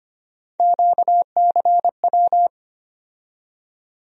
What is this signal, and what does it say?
Morse code, heard as a keyed tone, about 25 wpm, 700 Hz. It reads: QCW
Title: USA Call Sign Postfixes - 3 Letters 25wpm